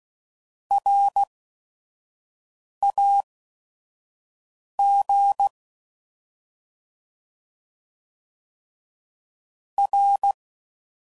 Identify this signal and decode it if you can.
Morse code, heard as a keyed tone, about 16 wpm, spaced as for 5 wpm, 800 Hz. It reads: RAG R
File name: Prefixes / By First Letter / Rs